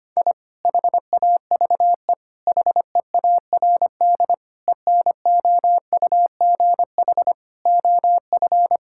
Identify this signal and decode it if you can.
Morse code, heard as a keyed tone, about 25 wpm, 700 Hz. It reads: I HAVE HEARD ENOUGH OF
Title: Sentences from Top 1000 Words - Review Part 6 - 25wpm